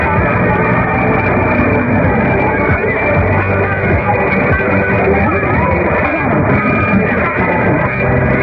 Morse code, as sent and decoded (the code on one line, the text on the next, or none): none